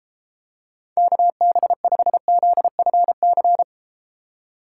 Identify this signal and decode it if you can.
Morse code, heard as a keyed tone, about 32 wpm, 700 Hz. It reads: KB5ZFC